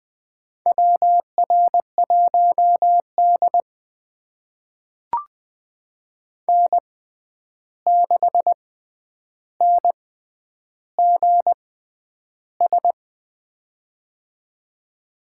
Morse code, heard as a keyed tone, about 21 wpm, spaced as for 10 wpm, 700 Hz. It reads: WR1D N6NGS
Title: USA Call Signs - 6x Character Spacing - 20wpm